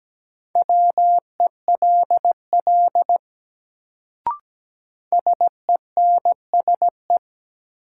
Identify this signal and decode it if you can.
Morse code, heard as a keyed tone, about 17 wpm, 700 Hz. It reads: WELL SENSE
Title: Sets of 2 Words - New Word Drill - 17wpm